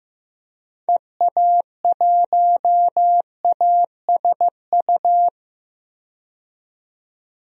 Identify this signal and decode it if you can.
Morse code, heard as a keyed tone, about 15 wpm, 700 Hz. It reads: EA1ASU
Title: DXCC-100 Call Signs - 6 Characters - 15wpm